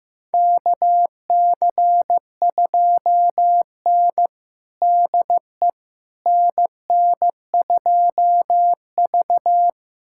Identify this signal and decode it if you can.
Morse code, heard as a keyed tone, about 15 wpm, 700 Hz. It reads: KC2N DE NN2V
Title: USA Callsign DE Callsign - 4 Characters - 15wpm